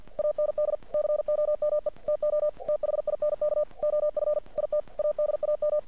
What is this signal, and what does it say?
Morse code, heard as a keyed tone, about 50 wpm, 600 Hz. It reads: AND COME TO THINK OF IT ADAM
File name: archived / CRSnet